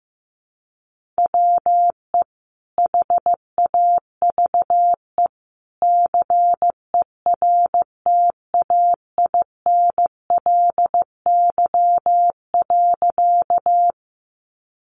Morse code, heard as a keyed tone, about 15 wpm, 700 Hz. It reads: WE HAVE CERTAINLY.
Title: Sentences from Top 900 Words 15wpm